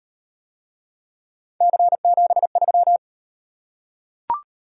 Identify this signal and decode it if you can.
Morse code, heard as a keyed tone, about 38 wpm, 700 Hz. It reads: C73